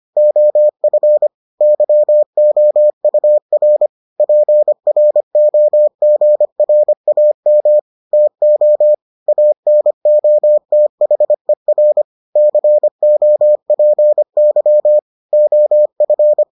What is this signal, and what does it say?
Morse code, heard as a keyed tone, about 25 wpm, 600 Hz. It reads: OF YOUR PROGRAM TO ANOTHER COPY OF